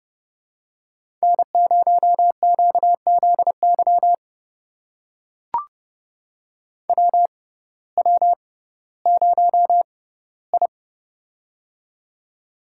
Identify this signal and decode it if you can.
Morse code, heard as a keyed tone, about 30 wpm, spaced as for 17 wpm, 700 Hz. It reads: N0QZY WW0I
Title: USA Call Signs - 6x Character Spacing - 30wpm